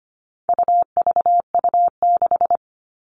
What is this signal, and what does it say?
Morse code, heard as a keyed tone, about 25 wpm, 700 Hz. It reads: UVU6
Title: All Cut Numbers - 4-Digit 25wpm